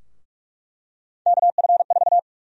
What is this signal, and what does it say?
Morse code, heard as a keyed tone, about 41 wpm, 700 Hz. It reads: KF4
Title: USA Call Sign Prefixes - Repeated 2x - 45wpm